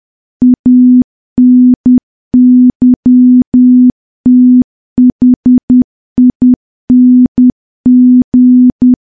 Morse code, heard as a keyed tone, about 10 wpm, 250 Hz. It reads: ANYTHING
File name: morse / morse-training-10WPM